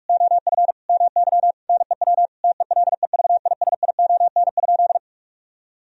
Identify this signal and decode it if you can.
Morse code, heard as a keyed tone, about 46 wpm, 700 Hz. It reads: OF MY NEW TELEVISION?
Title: Sentences from Top 1000 Words - Review Part 4 - 45wpm